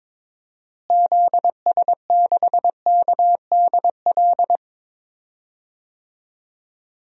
Morse code, heard as a keyed tone, about 21 wpm, 700 Hz. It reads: ZS6KDL
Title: DXCC-100 Call Signs - 6 Characters - Repeated 2x - 22wpm